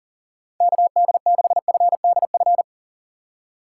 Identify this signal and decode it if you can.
Morse code, heard as a keyed tone, about 40 wpm, 700 Hz. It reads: KD6FDF
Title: USA Call Signs - 6 Characters - 8x Character Spacing - 40wpm